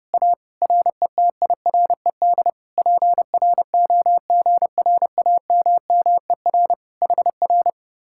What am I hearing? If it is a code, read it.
Morse code, heard as a keyed tone, about 29 wpm, 700 Hz. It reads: A RETIRED PROGRAMMER HR
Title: Speed-Racing Common QSO Phrases Encore - 30wpm